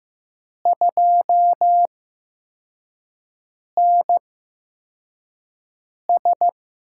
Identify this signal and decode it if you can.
Morse code, heard as a keyed tone, about 15 wpm, spaced as for 4 wpm, 700 Hz. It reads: 2NS
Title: USA Call Signs - 8x Character Spacing - 15wpm